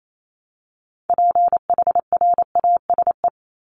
Morse code, heard as a keyed tone, about 28 wpm, 700 Hz. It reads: PHRASE
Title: Top 1501-1750 Words 28wpm